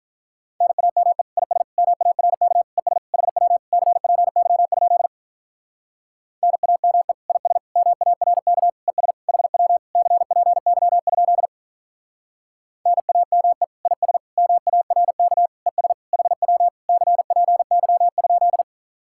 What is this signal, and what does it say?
Morse code, heard as a keyed tone, about 44 wpm, 700 Hz. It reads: NAME IS MARK ES HW CPY? NAME IS MARK ES HW CPY? NAME IS MARK ES HW CPY?